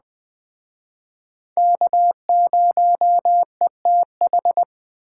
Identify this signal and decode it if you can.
Morse code, heard as a keyed tone, about 20 wpm, 700 Hz. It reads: K0ETH
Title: Speed Racing - USA 1x3 Call Signs 17wpm